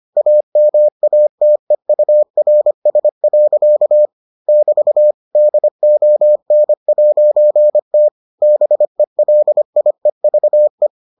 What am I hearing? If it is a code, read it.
Morse code, heard as a keyed tone, about 25 wpm, 600 Hz. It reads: AMATEURS. <BT> DON'T BELIEVE